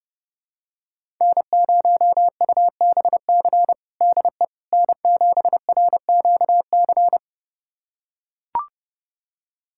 Morse code, heard as a keyed tone, about 32 wpm, 700 Hz. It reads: N0UBC DE N7RQC
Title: USA Callsign DE Callsign - 5 Characters - 30wpm